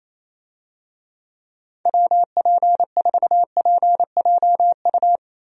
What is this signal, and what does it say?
Morse code, heard as a keyed tone, about 27 wpm, 700 Hz. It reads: WP4PJU
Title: USA Call Signs - 6 Characters - Repeated 2x - 28wpm